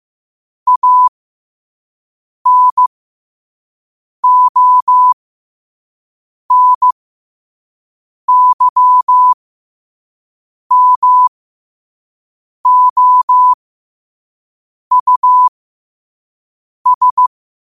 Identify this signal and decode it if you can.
Morse code, heard as a keyed tone, about 15 wpm, spaced as for 5 wpm, 1000 Hz. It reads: ANONYMOUS